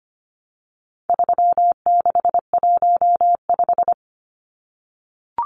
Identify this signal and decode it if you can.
Morse code, heard as a keyed tone, about 25 wpm, 700 Hz. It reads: 3615
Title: Numbers - 4-Digit 25wpm